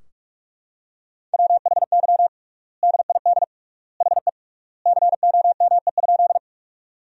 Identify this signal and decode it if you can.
Morse code, heard as a keyed tone, about 46 wpm, 700 Hz. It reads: WHY DID HE COME?
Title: 3x Word-Spacing - Sentences from Top 200 Words - 45wpm